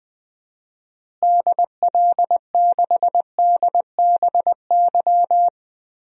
Morse code, heard as a keyed tone, about 20 wpm, 700 Hz. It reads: DL6DBY
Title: DXCC-100 Call Signs - 6 Characters - Repeated 3x - 20wpm